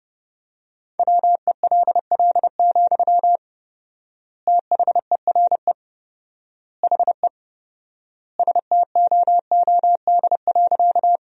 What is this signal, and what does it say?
Morse code, heard as a keyed tone, about 32 wpm, 700 Hz. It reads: WELL, THERE HE STOOD.
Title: 4x Word-Spacing - Sentences from Top 500 Words - 30wpm